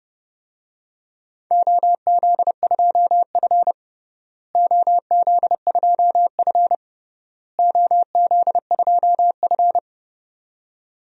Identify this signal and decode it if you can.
Morse code, heard as a keyed tone, about 30 wpm, 700 Hz. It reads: OZ2F OZ2F OZ2F
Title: DXCC-100 Call Signs - 4 Characters - Repeated 3x - 30wpm